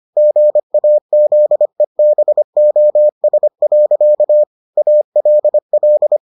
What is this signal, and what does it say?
Morse code, heard as a keyed tone, about 25 wpm, 600 Hz. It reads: GAZEBOS. ALL